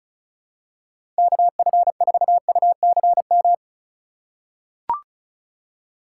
Morse code, heard as a keyed tone, about 36 wpm, 700 Hz. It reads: KF4UCM
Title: USA Call Signs - 6 Characters - 3x Character Spacing - 35wpm